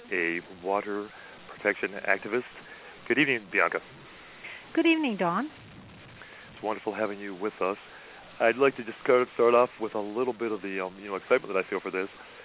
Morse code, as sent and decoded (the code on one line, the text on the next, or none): none